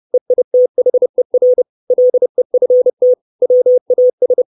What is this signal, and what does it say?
Morse code, heard as a keyed tone, about 30 wpm, 500 Hz. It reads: EITHER LEFT WAS
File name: 632 Words 30WPM